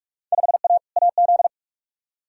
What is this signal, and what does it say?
Morse code, heard as a keyed tone, about 42 wpm, 700 Hz. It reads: 5A AZ
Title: Field Day Exchanges - 45wpm